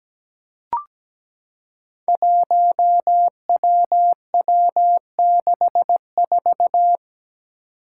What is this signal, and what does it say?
Morse code, heard as a keyed tone, about 17 wpm, 700 Hz. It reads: 1WW64
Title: Five Challenging Letter-Number Combinations 17wpm